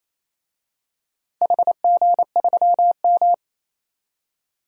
Morse code, heard as a keyed tone, about 28 wpm, 700 Hz. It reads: HG3M
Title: DXCC-100 Call Signs - 4 Characters - Repeated 3x - 28wpm